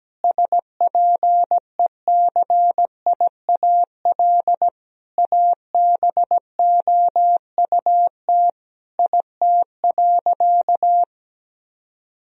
Morse code, heard as a keyed tone, about 17 wpm, 700 Hz. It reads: SPECIAL ABOUT IT.
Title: Sentences from Top 1000 Words - Review Part 3 - 17wpm